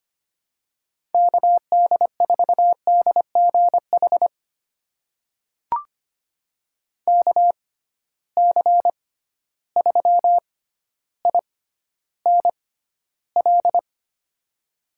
Morse code, heard as a keyed tone, about 25 wpm, spaced as for 13 wpm, 700 Hz. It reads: KD4DGH KC3INL
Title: USA Call Signs - 6 Characters - 6x Character Spacing - 25wpm